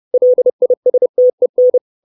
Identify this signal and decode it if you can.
Morse code, heard as a keyed tone, about 30 wpm, 500 Hz. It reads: LISTEN
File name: 632 Words 30WPM